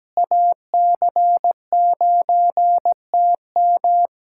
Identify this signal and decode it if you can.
Morse code, heard as a keyed tone, about 17 wpm, 700 Hz. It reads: AC9TM
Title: USA Callsign DE Callsign - 5 Characters - 17wpm